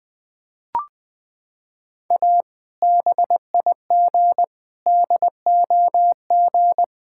Morse code, heard as a keyed tone, about 20 wpm, 700 Hz. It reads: A BIG DOG
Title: Sentences with 2-Word Phrases - 20wpm